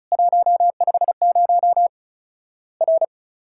Morse code, heard as a keyed tone, about 36 wpm, 700 Hz. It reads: 150 R
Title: Worked All Europe DX Contest - QTC Exchange - 35wpm